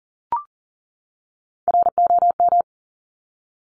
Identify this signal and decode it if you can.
Morse code, heard as a keyed tone, about 42 wpm, 700 Hz. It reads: ROM
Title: Top Three Letter Combinations in Words 40wpm